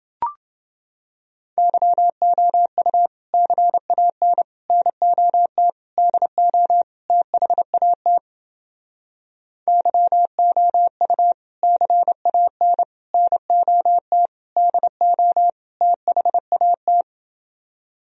Morse code, heard as a keyed tone, about 30 wpm, 700 Hz. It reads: YOU CAN NOT DO THAT YOU CAN NOT DO THAT